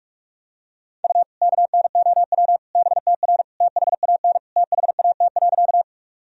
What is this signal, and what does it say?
Morse code, heard as a keyed tone, about 41 wpm, 700 Hz. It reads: U KNOW BTR THAN THAT.
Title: Sentences from Top 400 Words and Abbreviations 45wpm